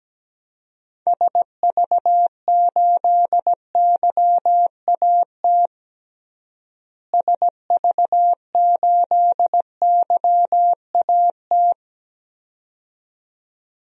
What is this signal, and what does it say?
Morse code, heard as a keyed tone, about 17 wpm, 700 Hz. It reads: SV8YAT SV8YAT
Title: DXCC-100 Call Signs - 6 Characters - Repeated 2x - 17wpm